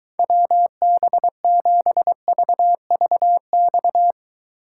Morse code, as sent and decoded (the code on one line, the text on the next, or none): .-- -... --... ...- ...- -..-
WB7VVX